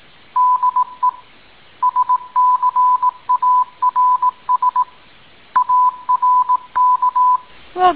Morse code, heard as a keyed tone, about 18 wpm, 1000 Hz. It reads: DE SCARS ARK